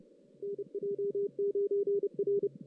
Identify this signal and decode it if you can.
Morse code, heard as a keyed tone, about 30 wpm, 400 Hz, with radio noise. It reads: NJ9R
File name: 2016 JAN CW NAQP / 2016 Jan naqp k5ot - 0-5